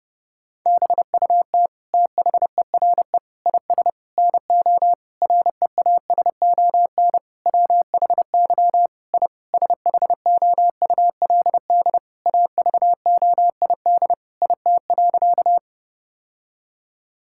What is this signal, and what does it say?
Morse code, heard as a keyed tone, about 32 wpm, 700 Hz. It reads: BUT THERE IS NO REASON WHY I SHOULD AVOID IT.
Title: Sentences from Top 1000 Words - Review Part 6 - 30wpm